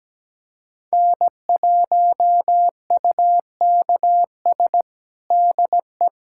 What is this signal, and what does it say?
Morse code, heard as a keyed tone, about 17 wpm, 700 Hz. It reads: N1UKS DE
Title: USA Callsign DE Callsign - 5 Characters - 17wpm